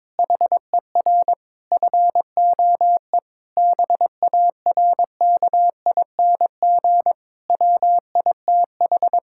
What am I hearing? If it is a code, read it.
Morse code, heard as a keyed tone, about 21 wpm, 700 Hz. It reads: HER FOE BARKING WITH
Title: Pangram Challenge v2 - 22wpm